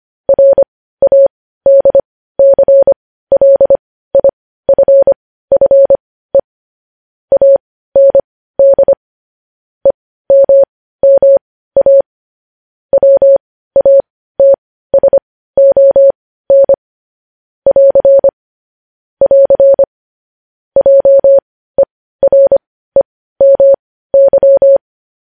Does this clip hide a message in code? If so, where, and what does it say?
Yes: Morse code from 0.3 to 24.8 s, heard as a keyed tone, about 25 wpm, spaced as for 15 wpm, 550 Hz. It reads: RADCLIFFE AND EMMA WATSON <AR> <AR> JEREMY